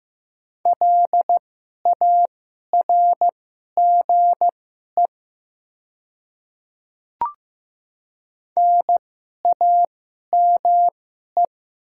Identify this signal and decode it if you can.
Morse code, heard as a keyed tone, about 15 wpm, spaced as for 11 wpm, 700 Hz. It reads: LARGE NAME
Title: 2x Character Spacing - Top 200 Words - 15wpm